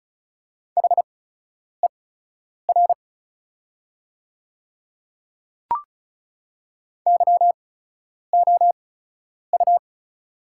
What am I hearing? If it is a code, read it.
Morse code, heard as a keyed tone, about 34 wpm, spaced as for 10 wpm, 700 Hz. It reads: HER YOU
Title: 8x Character Spacing - Top 100 Words - 35wpm